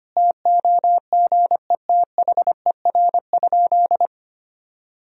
Morse code, heard as a keyed tone, about 25 wpm, 700 Hz. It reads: TOGETHER?
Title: Sentences from Top 1000 Words - Review Part 4 - 25wpm